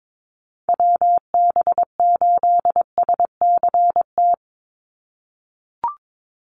Morse code, heard as a keyed tone, about 22 wpm, 700 Hz. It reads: WB8SCT
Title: Call Signs - 22wpm